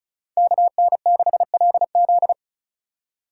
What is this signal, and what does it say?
Morse code, heard as a keyed tone, about 36 wpm, 700 Hz. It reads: KN6LZ
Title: USA Call Signs - 5 Characters - 2x Character Spacing - 35wpm